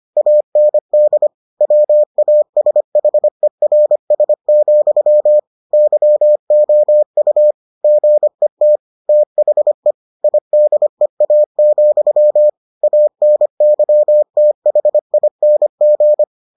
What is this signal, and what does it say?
Morse code, heard as a keyed tone, about 25 wpm, 600 Hz. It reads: AND WASHERS, YOU GET THE IDEA, ANYTHING